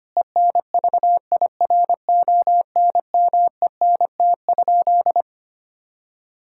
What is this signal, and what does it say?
Morse code, heard as a keyed tone, about 25 wpm, 700 Hz. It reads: ENVIRONMENT?
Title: Sentences from Top 1000 Words - Review Part 5 - 25wpm